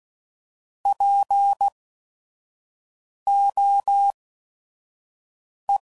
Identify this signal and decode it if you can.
Morse code, heard as a keyed tone, about 16 wpm, spaced as for 5 wpm, 800 Hz. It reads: POE
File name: Prefixes / By First Letter / Ps